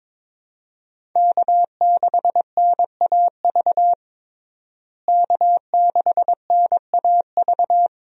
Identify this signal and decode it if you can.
Morse code, heard as a keyed tone, about 22 wpm, 700 Hz. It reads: K6NAV K6NAV
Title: USA Call Signs - 5 Characters - Repeated 3x - 22wpm